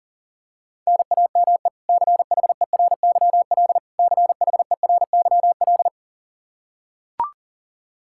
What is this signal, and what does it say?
Morse code, heard as a keyed tone, about 40 wpm, 700 Hz. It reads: NAME CHERYL CHERYL
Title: K1USN SST QSO Practice 40wpm